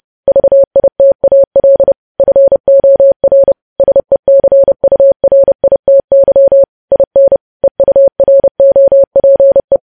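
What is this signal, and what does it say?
Morse code, heard as a keyed tone, about 30 wpm, 550 Hz. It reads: VITAL FOR SECURITY IN EUROPE
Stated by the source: News headlines sent in Morse at 30 WPM.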